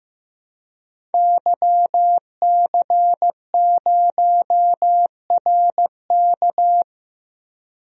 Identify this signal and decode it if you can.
Morse code, heard as a keyed tone, about 15 wpm, 700 Hz. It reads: YC0RK
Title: DXCC-100 Call Signs - 5 Characters - 15wpm